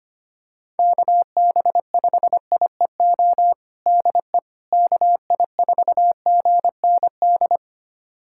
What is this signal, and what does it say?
Morse code, heard as a keyed tone, about 25 wpm, 700 Hz. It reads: KB5IEO DE KI4GND
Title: USA Callsign DE Callsign 25wpm